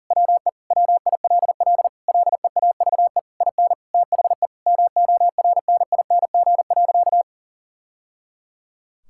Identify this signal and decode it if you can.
Morse code, heard as a keyed tone, about 37 wpm, 700 Hz. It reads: WE WILL LEAVE IN THE MORNING.
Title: Sentences from Top 1000 Words - Review Part 6 - 40wpm